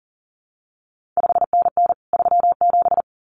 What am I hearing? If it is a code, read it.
Morse code, heard as a keyed tone, about 38 wpm, 700 Hz. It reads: 5NN 37
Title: CQ WW DX Contest Exchange Encore - 40wpm